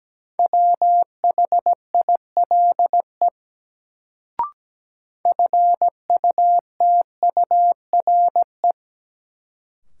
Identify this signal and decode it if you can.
Morse code, heard as a keyed tone, about 17 wpm, 700 Hz. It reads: WHILE FUTURE
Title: Sets of 3 Words - New Word Drill - 17wpm